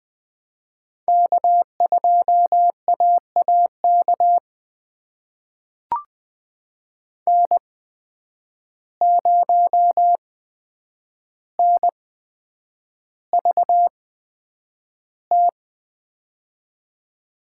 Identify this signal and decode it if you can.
Morse code, heard as a keyed tone, about 20 wpm, spaced as for 9 wpm, 700 Hz. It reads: K2AAK N0NVT
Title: USA Call Signs - 5 Characters - 8x Character Spacing - 20wpm